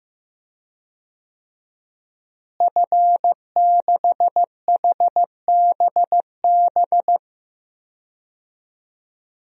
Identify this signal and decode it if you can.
Morse code, heard as a keyed tone, about 15 wpm, 700 Hz. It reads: F6HBB
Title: DXCC-100 Call Signs - 5 Characters - Repeated 2x - 15wpm